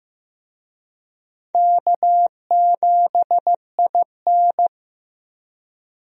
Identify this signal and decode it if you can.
Morse code, heard as a keyed tone, about 16 wpm, 700 Hz. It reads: K7IN
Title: USA Call Signs - 4 Characters - Repeated 2x - 15wpm